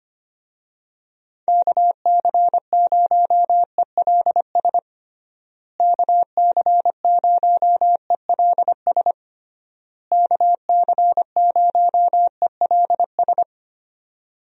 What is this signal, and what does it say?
Morse code, heard as a keyed tone, about 25 wpm, 700 Hz. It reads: KC0ELS KC0ELS KC0ELS
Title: USA Call Signs - 6 Characters - Repeated 3x - 25wpm